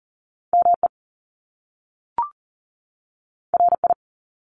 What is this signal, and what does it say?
Morse code, heard as a keyed tone, about 41 wpm, 700 Hz. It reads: ME RI
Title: Top Two Letter Combinations in Words 40wpm